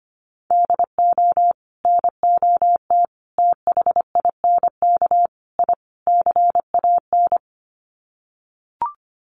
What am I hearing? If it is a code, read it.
Morse code, heard as a keyed tone, about 25 wpm, 700 Hz. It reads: DO NOT THINK I CAN
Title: Sets of 5 Words Encore2 25wpm